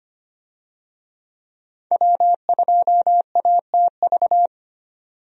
Morse code, heard as a keyed tone, about 25 wpm, 700 Hz. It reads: W2ATV
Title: USA Call Signs - 5 Characters - 25wpm